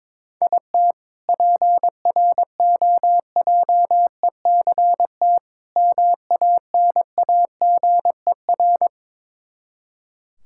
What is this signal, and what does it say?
Morse code, heard as a keyed tone, about 22 wpm, 700 Hz. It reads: IT PROJECT MANAGER
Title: K1USN SST QSO Practice 22wpm